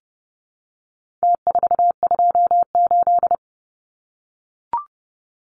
Morse code, heard as a keyed tone, about 30 wpm, 700 Hz. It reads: T428
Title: Numbers - Contest Style - 4-Digit 30wpm